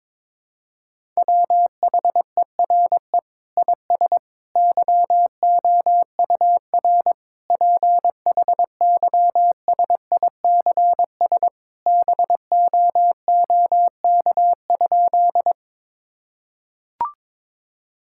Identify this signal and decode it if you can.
Morse code, heard as a keyed tone, about 22 wpm, 700 Hz. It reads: WHERE IS YOUR PHYSICS BOOK?
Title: Sentences from Top 1000 Words - Review Part 5 - 22wpm